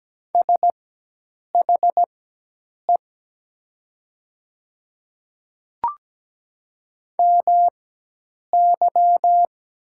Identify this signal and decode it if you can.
Morse code, heard as a keyed tone, about 17 wpm, spaced as for 8 wpm, 700 Hz. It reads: SHE MY
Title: 4x Character Spacing - Top 100 Words - 17wpm